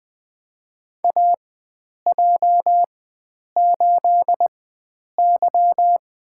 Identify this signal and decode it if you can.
Morse code, heard as a keyed tone, about 20 wpm, spaced as for 9 wpm, 700 Hz. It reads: AJ8Y
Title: USA Call Signs - 4 Characters - 4x Character Spacing - 20wpm